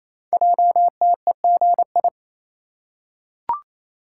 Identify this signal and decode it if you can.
Morse code, heard as a keyed tone, about 27 wpm, 700 Hz. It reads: JTEGI
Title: Code Groups - 5 Letters - 28wpm